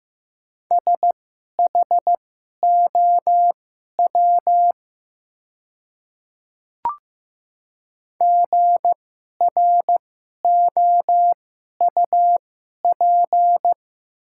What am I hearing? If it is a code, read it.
Morse code, heard as a keyed tone, about 15 wpm, spaced as for 11 wpm, 700 Hz. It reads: SHOW GROUP
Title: 2x Character Spacing - Top 300 Words - 15wpm